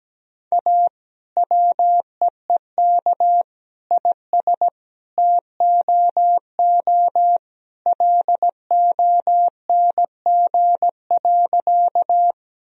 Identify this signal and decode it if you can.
Morse code, heard as a keyed tone, about 17 wpm, 700 Hz. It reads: A WEEK IS TOO LONG.
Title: Sentences from Top 1000 Words - Review Part 2 - 17wpm